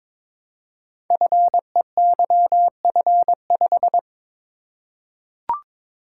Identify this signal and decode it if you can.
Morse code, heard as a keyed tone, about 23 wpm, 700 Hz. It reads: FEYF5
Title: Code Groups - 5 Letter:Numbers - 22wpm